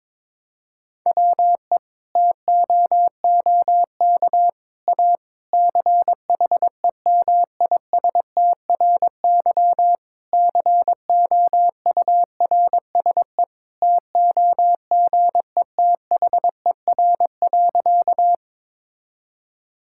Morse code, heard as a keyed tone, about 22 wpm, 700 Hz. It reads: WE TOOK A CHEMISTRY COURSE TOGETHER.
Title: Sentences from Top 1000 Words - Review Part 6 - 22wpm